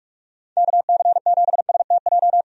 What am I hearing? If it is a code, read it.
Morse code, heard as a keyed tone, about 41 wpm, 700 Hz. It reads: KC7STJ
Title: USA Call Signs - 6 Characters - 45wpm